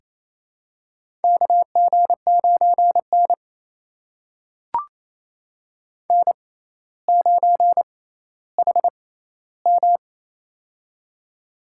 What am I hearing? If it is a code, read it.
Morse code, heard as a keyed tone, about 28 wpm, spaced as for 14 wpm, 700 Hz. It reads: KG9N N9HM
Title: USA Call Signs - 6x Character Spacing - 28wpm